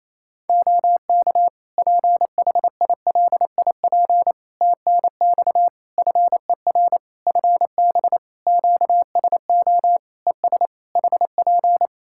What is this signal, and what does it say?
Morse code, heard as a keyed tone, about 28 wpm, 700 Hz. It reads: OK PHILIP TNX FER FB QSO ES HP
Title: Practice QSO - Exchange 4 - Ending - 28wpm